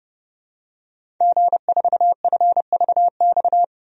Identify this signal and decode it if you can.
Morse code, heard as a keyed tone, about 30 wpm, 700 Hz. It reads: G4FVX
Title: DXCC-100 Call Signs - 5 Characters - Repeated 2x - 30wpm